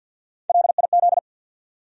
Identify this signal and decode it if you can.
Morse code, heard as a keyed tone, about 48 wpm, 700 Hz. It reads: LIZ
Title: USA Common Names - Max 3 Letters - 50wpm